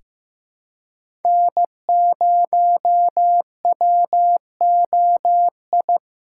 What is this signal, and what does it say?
Morse code, heard as a keyed tone, about 15 wpm, 700 Hz. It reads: N0WOI
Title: K1USN SST QSO Practice - 15wpm